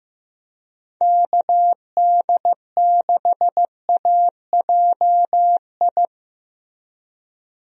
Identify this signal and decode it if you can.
Morse code, heard as a keyed tone, about 15 wpm, 700 Hz. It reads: KD6AJI
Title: USA Call Signs - 6 Characters - Repeated 2x - 15wpm